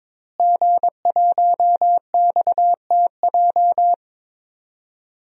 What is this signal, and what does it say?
Morse code, heard as a keyed tone, about 22 wpm, 700 Hz. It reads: G1XTJ
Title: DXCC-100 Call Signs - 5 Characters - Repeated 3x - 22wpm